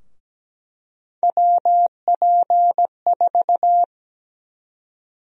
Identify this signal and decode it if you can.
Morse code, heard as a keyed tone, about 17 wpm, 700 Hz. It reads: WP4
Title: USA Call Sign Prefixes - 17wpm